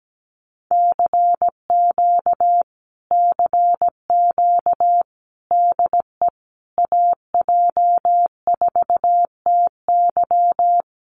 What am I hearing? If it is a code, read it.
Morse code, heard as a keyed tone, about 17 wpm, 700 Hz. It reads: CQ CQ DE AJ4TY
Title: Common QSO Phrases Encore 17wpm